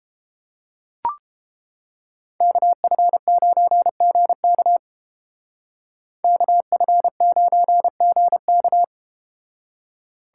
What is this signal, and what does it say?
Morse code, heard as a keyed tone, about 31 wpm, 700 Hz. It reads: KF9GK KF9GK